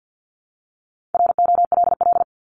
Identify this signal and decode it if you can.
Morse code, heard as a keyed tone, about 50 wpm, 700 Hz. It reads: ROLL
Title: Top 1751-2000 Words 50wpm